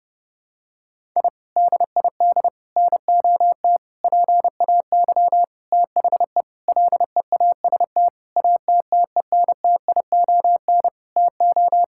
Morse code, heard as a keyed tone, about 30 wpm, 700 Hz. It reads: I DID NOT PAY THE LEAST ATTENTION TO
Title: Sentences from Top 1000 Words - Review Part 6 - 30wpm